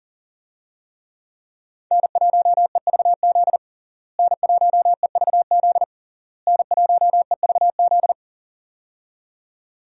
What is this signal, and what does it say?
Morse code, heard as a keyed tone, about 40 wpm, 700 Hz. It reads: N1EVZ N1EVZ N1EVZ
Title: USA Call Signs - 5 Characters - Repeated 3x - 40wpm